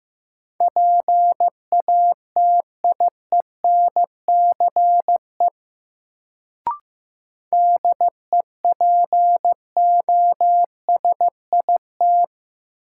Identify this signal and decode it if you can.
Morse code, heard as a keyed tone, about 15 wpm, 700 Hz. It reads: PATIENCE DEPOSIT
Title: Sentences from Top 1300 Words - New Word Drill - 15wpm